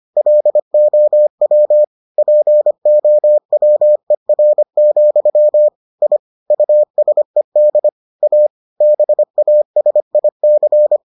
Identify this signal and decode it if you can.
Morse code, heard as a keyed tone, about 25 wpm, 600 Hz. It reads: LOW POWER, I USED A BASIC